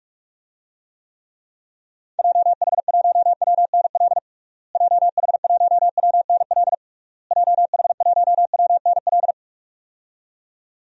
Morse code, heard as a keyed tone, about 41 wpm, 700 Hz. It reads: JH1WNL JH1WNL JH1WNL
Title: DXCC-100 Call Signs - 6 Characters - Repeated 3x - 45wpm